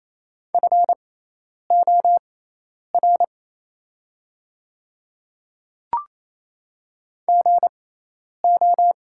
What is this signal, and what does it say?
Morse code, heard as a keyed tone, about 27 wpm, spaced as for 10 wpm, 690 Hz. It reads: FOR GO
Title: 6x Character Spacing - Top 100 Words - 28wpm